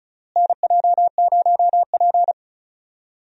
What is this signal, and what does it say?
Morse code, heard as a keyed tone, about 36 wpm, 700 Hz. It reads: NJ0P
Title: USA Call Signs - 4 Characters - Repeated 3x - 35wpm